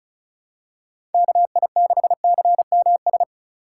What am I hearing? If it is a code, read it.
Morse code, heard as a keyed tone, about 34 wpm, 700 Hz. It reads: KI6CMS
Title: USA Call Signs - 3x Character Spacing - 35wpm